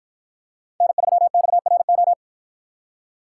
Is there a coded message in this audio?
Yes: Morse code from 0.8 to 2.1 s, heard as a keyed tone, about 53 wpm, 700 Hz. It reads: N3XRO